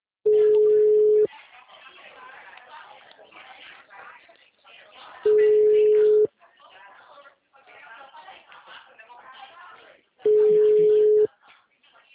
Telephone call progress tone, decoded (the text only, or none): ringback tone